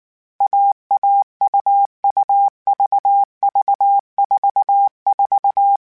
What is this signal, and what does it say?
Morse code, heard as a keyed tone, about 19 wpm, 800 Hz. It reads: AA UU VV 44